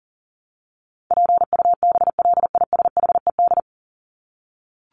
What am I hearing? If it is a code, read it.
Morse code, heard as a keyed tone, about 40 wpm, 700 Hz. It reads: PUBLISHED
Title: ED Suffix - Top 100 Words with Base Words 40wpm